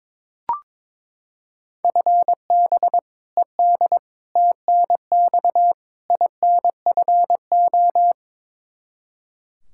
Morse code, heard as a keyed tone, about 22 wpm, 700 Hz. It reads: FB ED TNX INFO
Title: K1USN SST QSO Practice 22wpm